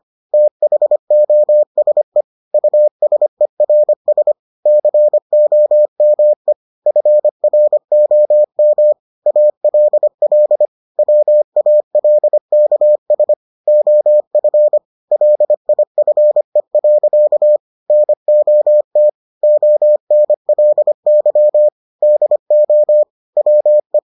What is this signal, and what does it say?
Morse code, heard as a keyed tone, about 25 wpm, 600 Hz. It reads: THOSE USERS COME FROM ALL WALKS OF LIFE. NOT ONLY DO WE